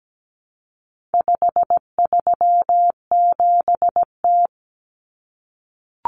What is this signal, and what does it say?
Morse code, heard as a keyed tone, about 17 wpm, 700 Hz. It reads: 537T
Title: Common Cut Numbers - 4-Digit 17wpm